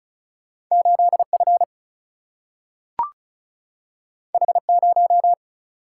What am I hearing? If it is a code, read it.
Morse code, heard as a keyed tone, about 36 wpm, 700 Hz. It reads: 8 F H 0
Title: Single-Letter Single-Number 35wpm